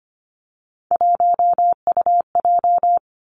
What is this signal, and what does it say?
Morse code, heard as a keyed tone, about 25 wpm, 700 Hz. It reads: 1UJ
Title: Three Challenging Letter-Number Combinations 25wpm